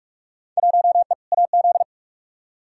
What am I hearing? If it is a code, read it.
Morse code, heard as a keyed tone, about 46 wpm, 700 Hz. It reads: 1E AZ
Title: Field Day Exchanges - 45wpm